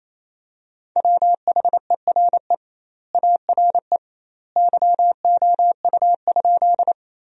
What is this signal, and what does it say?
Morse code, heard as a keyed tone, about 28 wpm, 700 Hz. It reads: WHERE ARE YOU?
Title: 2x Word-Spacing - Sentences from Top 200 Words - 28wpm